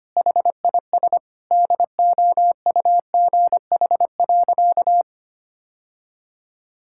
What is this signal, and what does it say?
Morse code, heard as a keyed tone, about 25 wpm, 700 Hz. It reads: HIS DOUGH.
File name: practice / 25wpm / All-Age Appropriate Jokes - 25wpm